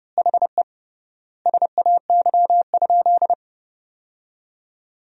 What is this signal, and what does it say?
Morse code, heard as a keyed tone, about 32 wpm, 700 Hz. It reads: HE SAY?
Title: 3x Word-Spacing - Sentences from Top 200 Words - 30wpm